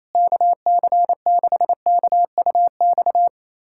Morse code, heard as a keyed tone, about 28 wpm, 700 Hz. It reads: KC6KUX
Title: USA Callsign DE Callsign - 6 Characters - 28wpm